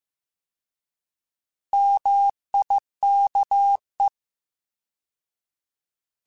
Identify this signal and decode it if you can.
Morse code, heard as a keyed tone, about 15 wpm, 800 Hz. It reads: MIKE